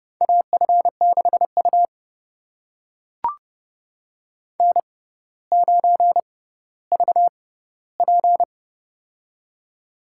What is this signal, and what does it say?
Morse code, heard as a keyed tone, about 30 wpm, spaced as for 15 wpm, 700 Hz. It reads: AF6U N9VP
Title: USA Call Signs - 4 Characters - 6x Character Spacing - 30wpm